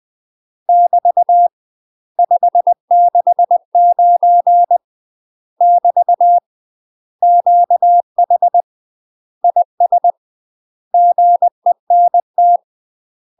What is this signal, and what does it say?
Morse code, heard as a keyed tone, about 20 wpm, 700 Hz. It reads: <BT> 569 <BT> QH IS GENT
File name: UK MASTER QSOs TRAINING/20WPM